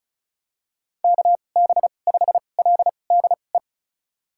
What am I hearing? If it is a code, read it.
Morse code, heard as a keyed tone, about 36 wpm, spaced as for 25 wpm, 700 Hz. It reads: KB5LDE
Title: USA Call Signs - 6 Characters - 2x Character Spacing - 35wpm